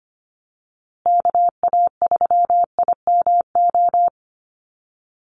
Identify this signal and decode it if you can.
Morse code, heard as a keyed tone, about 25 wpm, 700 Hz. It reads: KA3IMO
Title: USA Call Signs Repeated 3x 25wpm